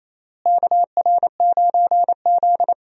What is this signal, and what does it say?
Morse code, heard as a keyed tone, about 28 wpm, 700 Hz. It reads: KR9Z
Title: USA Call Signs - 4 Characters - Repeated 3x - 28wpm